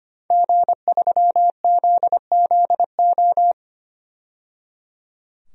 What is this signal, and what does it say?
Morse code, heard as a keyed tone, about 25 wpm, 700 Hz. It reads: G3ZZO
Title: DXCC-100 Call Signs - 5 Characters - 25wpm